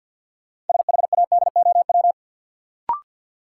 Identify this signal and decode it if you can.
Morse code, heard as a keyed tone, about 50 wpm, 700 Hz. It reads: SHADOW
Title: Sets of 5 Words Encore - New Word Drill - 50wpm